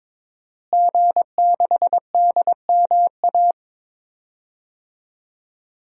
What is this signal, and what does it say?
Morse code, heard as a keyed tone, about 22 wpm, 700 Hz. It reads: G6DMA
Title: DXCC-100 Call Signs - 5 Characters - Repeated 3x - 22wpm